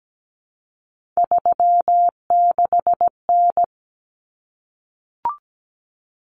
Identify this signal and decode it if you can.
Morse code, heard as a keyed tone, about 17 wpm, 690 Hz. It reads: 36N
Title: Common Cut Numbers - 3-Digit 17wpm